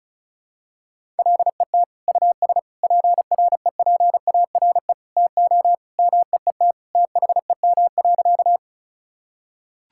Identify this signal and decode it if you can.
Morse code, heard as a keyed tone, about 36 wpm, 700 Hz. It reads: LET US PREPARE TO MEET THEM.
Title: Sentences from Top 1000 Words - Review Part 6 - 35wpm